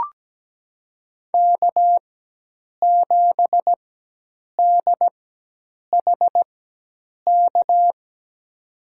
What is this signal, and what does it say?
Morse code, heard as a keyed tone, about 17 wpm, spaced as for 8 wpm, 700 Hz. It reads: K7DHK
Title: USA Call Signs - 5 Characters - 4x Character Spacing - 17wpm